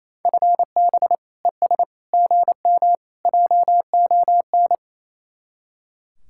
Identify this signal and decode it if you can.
Morse code, heard as a keyed tone, about 35 wpm, 700 Hz. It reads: FB ES GM JON
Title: Speed-Racing Common QSO Phrases Encore - 28wpm